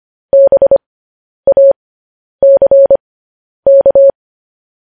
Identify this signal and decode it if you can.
Morse code, heard as a keyed tone, about 25 wpm, spaced as for 10 wpm, 550 Hz. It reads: BACK
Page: news headlines in morse code at 10 wpm